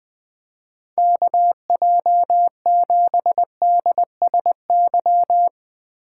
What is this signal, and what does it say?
Morse code, heard as a keyed tone, about 21 wpm, 700 Hz. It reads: KJ7DSY
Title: USA Call Signs - 6 Characters - 8x Character Spacing - 20wpm